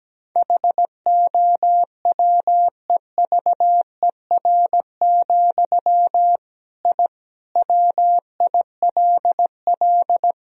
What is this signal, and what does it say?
Morse code, heard as a keyed tone, about 17 wpm, 700 Hz. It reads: HOWEVER, I WILL
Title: Sentences from Top 1000 Words - Review Part 3 - 17wpm